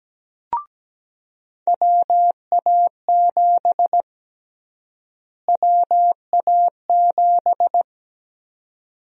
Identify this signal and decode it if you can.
Morse code, heard as a keyed tone, about 17 wpm, 700 Hz. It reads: WA7 WA7
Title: USA Call Sign Prefixes - Repeated 2x - 17wpm